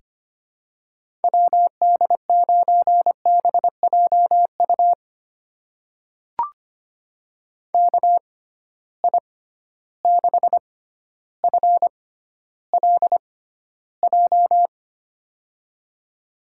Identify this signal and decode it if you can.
Morse code, heard as a keyed tone, about 25 wpm, spaced as for 13 wpm, 700 Hz. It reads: WD9BJU KI6FLJ
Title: USA Call Signs - 6 Characters - 6x Character Spacing - 25wpm